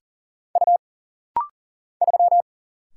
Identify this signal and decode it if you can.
Morse code, heard as a keyed tone, about 39 wpm, 700 Hz. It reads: U 3
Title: Single Letter-Number Alt - Rapid-Fire 40wpm